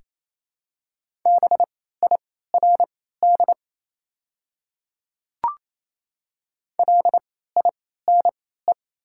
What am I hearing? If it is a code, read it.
Morse code, heard as a keyed tone, about 28 wpm, spaced as for 16 wpm, 700 Hz. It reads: BIRD LINE
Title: 3x Character Spacing - Top 500 Words - 28wpm